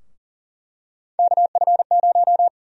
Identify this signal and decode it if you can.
Morse code, heard as a keyed tone, about 40 wpm, 700 Hz. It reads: KF0
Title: USA Call Sign Prefixes - Repeated 2x - 40wpm